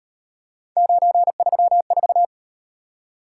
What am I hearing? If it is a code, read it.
Morse code, heard as a keyed tone, about 38 wpm, 700 Hz. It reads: 934